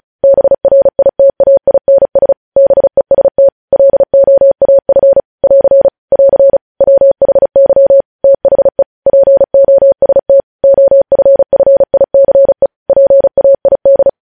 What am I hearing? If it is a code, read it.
Morse code, heard as a keyed tone, about 36 wpm, 550 Hz. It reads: BRITAINS BEST LOAF <AR> <AR> WHY THE POST OFFICE PAID
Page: News headlines in Morse code at 35 wpm